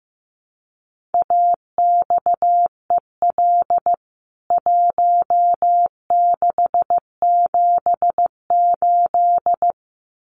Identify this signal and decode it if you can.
Morse code, heard as a keyed tone, about 14 wpm, 700 Hz. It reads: AXEL 1678
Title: CWT Exchange 15wpm